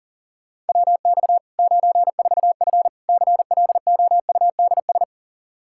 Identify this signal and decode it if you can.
Morse code, heard as a keyed tone, about 40 wpm, 700 Hz. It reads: WX 94F CLOUDS
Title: Practice QSO - Exchange 2 - Info - 40wpm